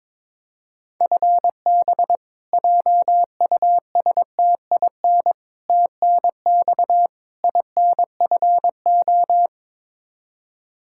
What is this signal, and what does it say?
Morse code, heard as a keyed tone, about 22 wpm, 700 Hz. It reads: FB JUSTIN TNX INFO
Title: Practice QSO - Exchange 3 - More Info - 22wpm